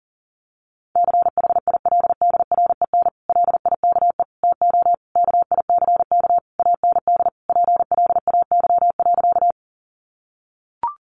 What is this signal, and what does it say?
Morse code, heard as a keyed tone, about 40 wpm, 700 Hz. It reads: CHILDREN LIKE TO KICK AND PLAY.
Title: Sentences from Top 1300 Words 40wpm